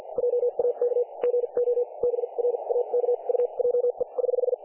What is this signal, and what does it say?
Morse code, heard as a keyed tone, about 50 wpm, spaced as for 39 wpm, 500 Hz, with radio noise. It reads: JAK GO BRAKUJE <HH>
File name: dlazaawansowanych / nagraniazpasma / websdr recording 3564.7kHz